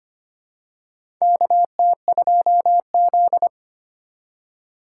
Morse code, heard as a keyed tone, about 25 wpm, 700 Hz. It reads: KT2Z
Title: USA Call Signs - 4 Characters - 2x Character Spacing - 25wpm